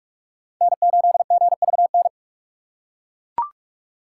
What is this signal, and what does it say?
Morse code, heard as a keyed tone, about 42 wpm, 700 Hz. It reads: N8GVN
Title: USA Call Signs - 5 Characters - 2x Character Spacing - 45wpm